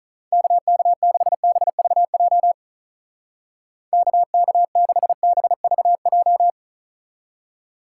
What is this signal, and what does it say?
Morse code, heard as a keyed tone, about 42 wpm, 700 Hz. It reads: KK6BVJ KK6BVJ
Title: Speed Racing - USA 2x3 Call Signs 35wpm